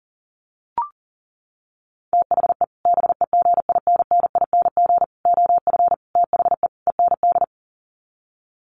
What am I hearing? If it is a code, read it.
Morse code, heard as a keyed tone, about 40 wpm, 700 Hz. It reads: THE BEGINNING OF THE END
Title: Sets of 5 Words Encore2 40wpm